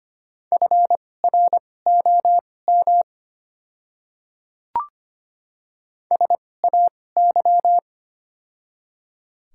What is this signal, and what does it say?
Morse code, heard as a keyed tone, about 25 wpm, spaced as for 18 wpm, 700 Hz. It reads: FROM SAY
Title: 2x Character Spacing - Top 300 Words - 25wpm